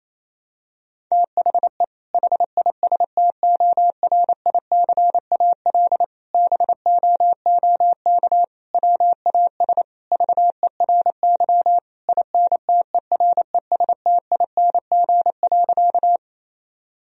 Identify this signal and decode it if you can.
Morse code, heard as a keyed tone, about 28 wpm, 700 Hz. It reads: THE HISTORICAL BOOK WAS VERY INTERESTING.
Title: Sentences from Top 1000 Words - Review Part 4 - 28wpm